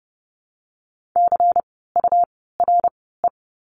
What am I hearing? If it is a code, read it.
Morse code, heard as a keyed tone, about 30 wpm, spaced as for 17 wpm, 700 Hz. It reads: CURE
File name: Lesson 17f - Letter u - Words 30wpm